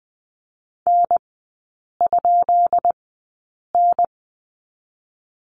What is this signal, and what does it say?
Morse code, heard as a keyed tone, about 20 wpm, spaced as for 8 wpm, 700 Hz. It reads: N?N
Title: 133 Lesson 22e - Question Mark - Three Characters 20wpm